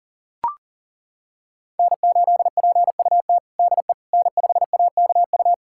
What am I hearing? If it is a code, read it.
Morse code, heard as a keyed tone, about 40 wpm, 700 Hz. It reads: N8PUT DE N5AKU
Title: USA Callsign DE Callsign - 5 Characters - 40wpm